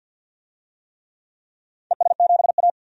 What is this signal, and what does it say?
Morse code, heard as a keyed tone, about 51 wpm, 700 Hz. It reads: ES7A
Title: DXCC-100 Call Signs - 4 Characters - 50wpm